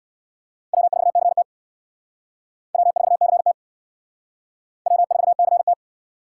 Morse code, heard as a keyed tone, about 60 wpm, 700 Hz. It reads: W4CT W4CT W4CT
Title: Speed Racing - USA 1x2 Call Signs 50wpm